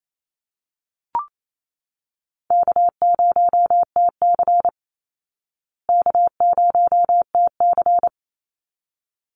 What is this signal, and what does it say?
Morse code, heard as a keyed tone, about 28 wpm, 700 Hz. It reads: K0TC K0TC
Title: Easy Call Signs Repeated 2x - 28wpm